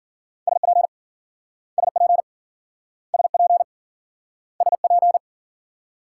Morse code, heard as a keyed tone, about 53 wpm, 700 Hz. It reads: SP SP SP SP